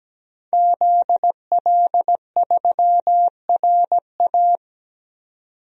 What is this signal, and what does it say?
Morse code, heard as a keyed tone, about 17 wpm, 700 Hz. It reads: ZL3RA
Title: DXCC-100 Call Signs - 5 Characters - 17wpm